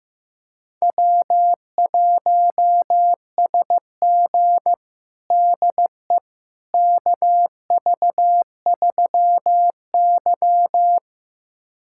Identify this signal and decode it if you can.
Morse code, heard as a keyed tone, about 15 wpm, 700 Hz. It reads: W1SG DE KV3Y
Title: USA Callsign DE Callsign - 4 Characters - 15wpm